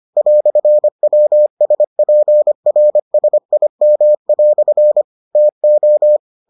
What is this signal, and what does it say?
Morse code, heard as a keyed tone, about 25 wpm, 600 Hz. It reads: "WSPRSIM" TO